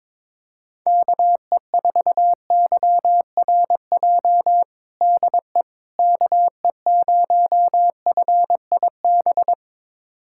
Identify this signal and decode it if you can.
Morse code, heard as a keyed tone, about 22 wpm, 700 Hz. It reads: KE4YRJ DE KE0FIB
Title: USA Callsign DE Callsign - 6 Characters - 22wpm